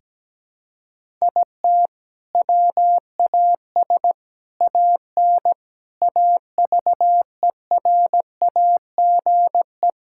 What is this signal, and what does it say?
Morse code, heard as a keyed tone, about 17 wpm, 700 Hz. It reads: IT WAS AN AVERAGE
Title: Sentences from Top 1000 Words - Review Part 6 - 17wpm